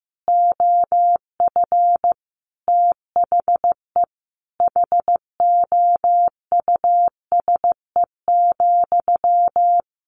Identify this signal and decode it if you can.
Morse code, heard as a keyed tone, about 15 wpm, 700 Hz. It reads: OF THE HOUSE,
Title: Sentences from Top 1200 Words 15wpm